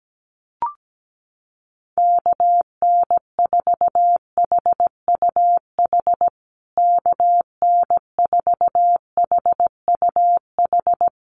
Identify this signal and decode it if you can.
Morse code, heard as a keyed tone, about 17 wpm, 700 Hz. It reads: KN4HUH KN4HUH
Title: USA Call Signs Repeated 3x 17wpm